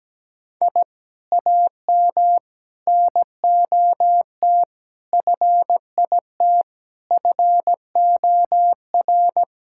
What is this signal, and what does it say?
Morse code, heard as a keyed tone, about 17 wpm, 700 Hz. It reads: I AM NOT FIT FOR
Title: Sentences from Top 1000 Words - Review Part 5 - 17wpm